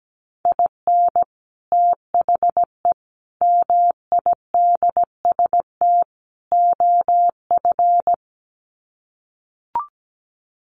Morse code, heard as a keyed tone, about 17 wpm, 700 Hz. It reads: IN THE MIDST OF
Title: Sets of 4 Words 17wpm